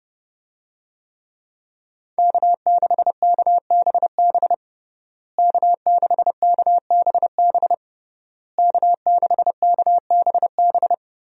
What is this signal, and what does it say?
Morse code, heard as a keyed tone, about 30 wpm, 700 Hz. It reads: K6KBB K6KBB K6KBB
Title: USA Call Signs - 5 Characters - Repeated 3x - 30wpm